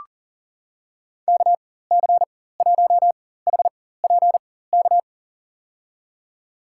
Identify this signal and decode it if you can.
Morse code, heard as a keyed tone, about 40 wpm, spaced as for 19 wpm, 700 Hz. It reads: KC1HPK
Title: USA Call Signs - 4x Character Spacing - 40wpm